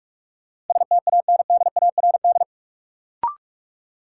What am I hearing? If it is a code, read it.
Morse code, heard as a keyed tone, about 43 wpm, 700 Hz. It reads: STANDARD
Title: Sentences from Top 1000 Words - Review Part 6 - New Word Drill - 45wpm